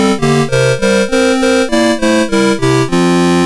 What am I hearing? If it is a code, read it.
There is no code.